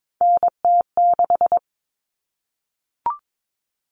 Morse code, heard as a keyed tone, about 22 wpm, 700 Hz. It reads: NT6
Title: All Cut Numbers - 3-Digit 22wpm